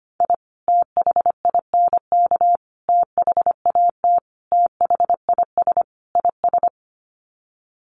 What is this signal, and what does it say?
Morse code, heard as a keyed tone, about 25 wpm, 700 Hz. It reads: I THINK THAT THIS IS
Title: Sets of 5 Words Encore2 25wpm